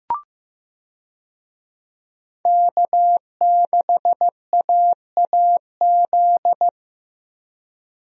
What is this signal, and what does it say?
Morse code, heard as a keyed tone, about 14 wpm, 700 Hz. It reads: K6AAZ
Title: USA Call Signs - 5 Characters - Repeated 3x - 15wpm